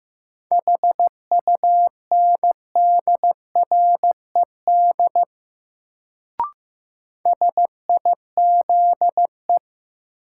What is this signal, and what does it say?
Morse code, heard as a keyed tone, about 15 wpm, 700 Hz. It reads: HUNDRED SIZE
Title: Sentences from Top 400 Words - New Word Drill - 15wpm